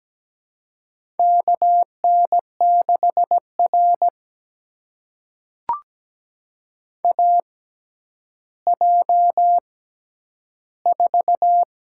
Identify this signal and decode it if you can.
Morse code, heard as a keyed tone, about 17 wpm, spaced as for 10 wpm, 700 Hz. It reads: KN6R AJ4
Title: USA Call Signs - 4 Characters - 6x Character Spacing - 17wpm